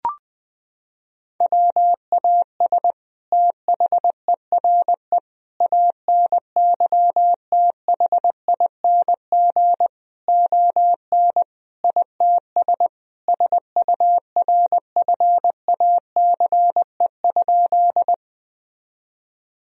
Morse code, heard as a keyed tone, about 20 wpm, 700 Hz. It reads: WAS THERE ANYTHING ON ITS SURFACE?
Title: Sentences from Top 1000 Words - Review Part 3 - 20wpm